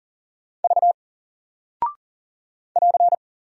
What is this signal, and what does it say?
Morse code, heard as a keyed tone, about 39 wpm, 700 Hz. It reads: V <AR>
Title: Single Character (Letter:Number:Punctuation:Prosign) 40wpm